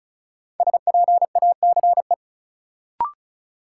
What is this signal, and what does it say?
Morse code, heard as a keyed tone, about 36 wpm, 700 Hz. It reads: SPACE
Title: Sentences from Top 1000 Words - Review Part 5 - New Word Drill - 35wpm